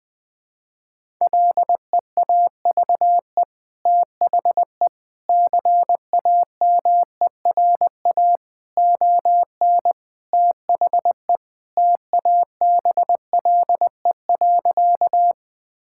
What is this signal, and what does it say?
Morse code, heard as a keyed tone, about 21 wpm, 700 Hz. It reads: LEAVE THE CAMERA ON THE TABLE.
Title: Sentences from Top 1000 Words - Review Part 5 - 20wpm